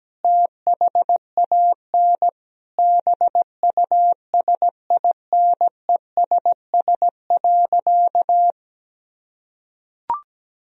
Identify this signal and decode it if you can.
Morse code, heard as a keyed tone, about 17 wpm, 700 Hz. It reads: THAN BUSINESS.
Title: Sentences from Top 1000 Words - Review Part 2 - 17wpm